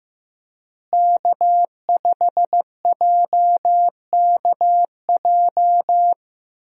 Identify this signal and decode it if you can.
Morse code, heard as a keyed tone, about 16 wpm, 700 Hz. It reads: K5JKJ
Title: USA Call Signs - 5 Characters - Repeated 3x - 15wpm